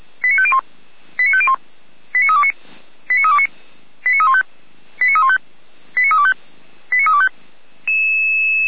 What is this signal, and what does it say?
Five-tone selective call (ZVEI), heard as quick runs of five tones, tones of about 70 ms. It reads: 89501 89501 89329 89329 89315 89315 89325 89325